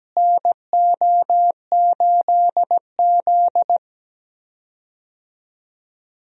Morse code, heard as a keyed tone, about 17 wpm, 700 Hz. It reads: NO8Z
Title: USA Call Signs - 4 Characters - Repeated 2x - 17wpm